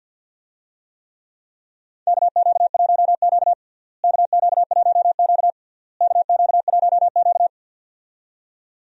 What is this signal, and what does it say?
Morse code, heard as a keyed tone, about 51 wpm, 700 Hz. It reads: KQ1Q KQ1Q KQ1Q
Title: USA Call Signs - 4 Characters - Repeated 3x - 50wpm